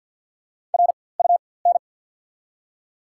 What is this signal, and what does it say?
Morse code, heard as a keyed tone, about 54 wpm, spaced as for 24 wpm, 690 Hz. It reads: RUN
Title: 4x Character Spacing - Top 500 Words - 50wpm